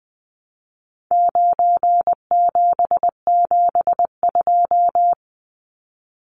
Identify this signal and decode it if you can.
Morse code, heard as a keyed tone, about 20 wpm, 700 Hz. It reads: 9772
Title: Numbers - 4-Digit 20wpm